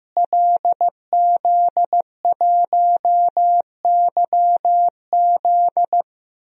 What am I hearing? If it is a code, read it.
Morse code, heard as a keyed tone, about 15 wpm, 700 Hz. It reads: LZ1YZ
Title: DXCC-100 Call Signs - 5 Characters - Repeated 2x - 15wpm